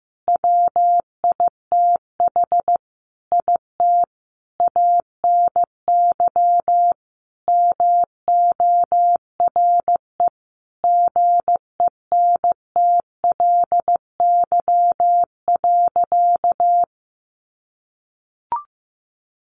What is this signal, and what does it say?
Morse code, heard as a keyed tone, about 15 wpm, 700 Hz. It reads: WITH IT ANY MORE GENTLY.
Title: Sentences from Top 1500 Words 15wpm